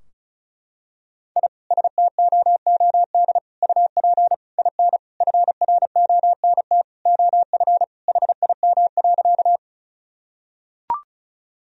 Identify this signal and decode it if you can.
Morse code, heard as a keyed tone, about 34 wpm, 700 Hz. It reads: I STOOD UP IN FRONT OF HIM.
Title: Sentences from Top 1000 Words - Review Part 2 - 35wpm